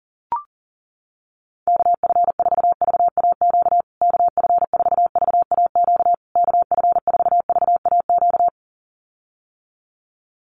Morse code, heard as a keyed tone, about 37 wpm, 700 Hz. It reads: KF4VAQ KF4VAQ KF4VAQ
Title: USA Call Signs Repeated 3x 40wpm